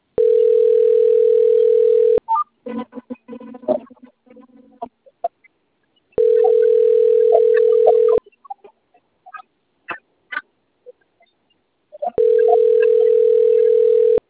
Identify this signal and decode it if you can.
Ringback tone (North America), 440 + 480 Hz.
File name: rg-1600-unknown- - - .12268